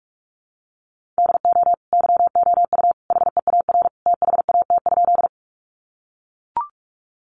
Morse code, heard as a keyed tone, about 44 wpm, 700 Hz. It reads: DO YOU HEAR THAT?
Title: Sentences from Top 300 Words 45wpm